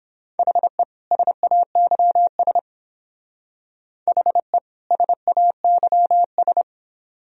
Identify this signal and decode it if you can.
Morse code, heard as a keyed tone, about 28 wpm, 700 Hz. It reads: HE SAYS HE SAYS